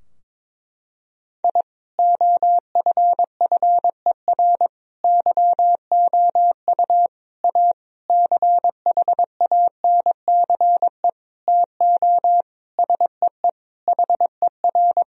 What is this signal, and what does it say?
Morse code, heard as a keyed tone, about 22 wpm, 700 Hz. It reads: I OFFER YOU A CHANCE TO SEE HER
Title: Sentences from Top 1000 Words - Review Part 4 - 22wpm